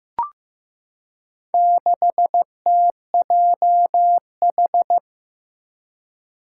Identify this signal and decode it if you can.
Morse code, heard as a keyed tone, about 15 wpm, 700 Hz. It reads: 6TJH
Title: Four Challenging Letter-Number Combinations 15wpm